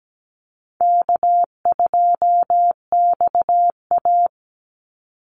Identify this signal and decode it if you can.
Morse code, heard as a keyed tone, about 17 wpm, 700 Hz. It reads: K2XA
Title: Call Signs - 17wpm